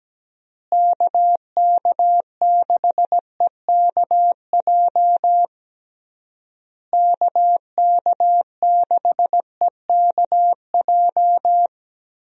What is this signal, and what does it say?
Morse code, heard as a keyed tone, about 17 wpm, 700 Hz. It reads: KK6EKJ KK6EKJ
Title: USA Call Signs - 6 Characters - Repeated 3x - 17wpm